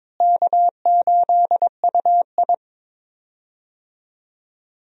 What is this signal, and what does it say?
Morse code, heard as a keyed tone, about 22 wpm, 700 Hz. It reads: K8UI
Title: USA Call Signs - 4 Characters - Repeated 3x - 22wpm